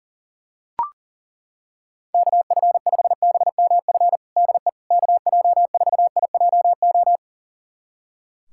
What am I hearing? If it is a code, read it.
Morse code, heard as a keyed tone, about 40 wpm, 690 Hz. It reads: KF5BMF DE KJ4IJO
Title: USA Callsign DE Callsign - 6 Characters - 40wpm